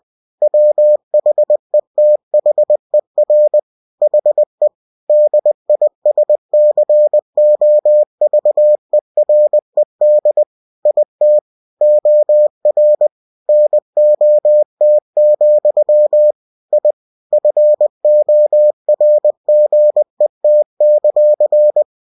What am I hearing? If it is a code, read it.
Morse code, heard as a keyed tone, about 20 wpm, 600 Hz. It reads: WHETHER HE DISCOVERED IT OR NOT, I FORGET;
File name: cw-books / winnie-the-pooh / winnie-20wpm-page-0093